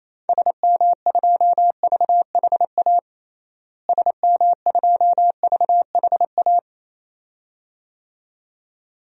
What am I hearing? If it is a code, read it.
Morse code, heard as a keyed tone, about 28 wpm, 700 Hz. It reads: SM2VHA SM2VHA
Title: DXCC-100 Call Signs - 6 Characters - Repeated 2x - 28wpm